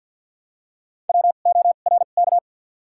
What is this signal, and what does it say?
Morse code, heard as a keyed tone, about 53 wpm, spaced as for 37 wpm, 700 Hz. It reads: WORK
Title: 2x Character Spacing - Top 200 Words - 50wpm